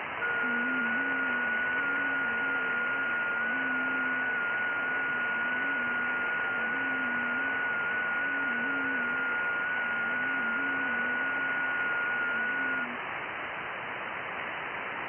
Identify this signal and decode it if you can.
FT8.
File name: IC-7100